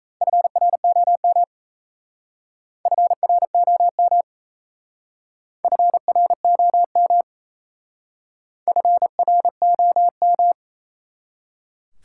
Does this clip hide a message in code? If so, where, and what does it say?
Yes: Morse code from 0.2 to 10.5 s, heard as a keyed tone, about 37 wpm, 700 Hz. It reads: FROM FROM FROM FROM